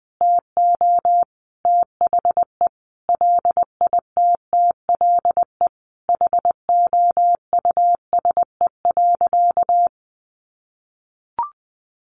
Morse code, heard as a keyed tone, about 20 wpm, 700 Hz. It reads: TO THE LITTLE HOUSE.
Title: Sentences from Top 200 Words 20wpm